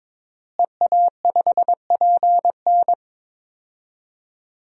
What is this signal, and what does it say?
Morse code, heard as a keyed tone, about 22 wpm, 700 Hz. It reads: EA5PN
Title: DXCC-100 Call Signs - 5 Characters - Repeated 2x - 22wpm